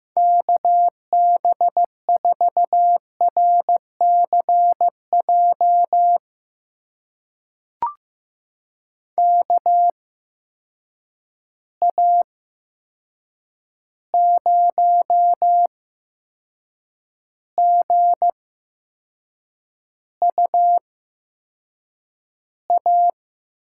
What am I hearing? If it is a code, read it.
Morse code, heard as a keyed tone, about 15 wpm, spaced as for 6 wpm, 700 Hz. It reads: KB4RCJ KA0GUA
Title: USA Call Signs - 6 Characters - 8x Character Spacing - 15wpm